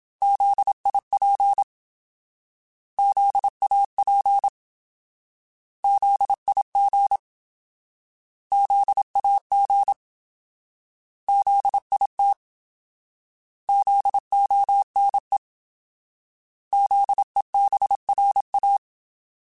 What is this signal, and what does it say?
Morse code, heard as a keyed tone, about 26 wpm, 800 Hz. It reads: ZIP ZAP ZIG ZAG ZIT ZONE ZEBRA
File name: Alphabetical / Alphabetical mp3s / Zs @25